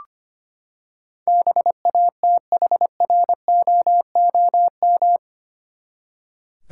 Morse code, heard as a keyed tone, about 25 wpm, 700 Hz. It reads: BATHROOM
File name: Eight Letter Words - 25wpm